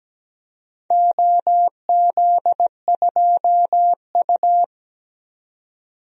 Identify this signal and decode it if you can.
Morse code, heard as a keyed tone, about 17 wpm, 700 Hz. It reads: OZ2U
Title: DXCC-100 Call Signs - 4 Characters - 17wpm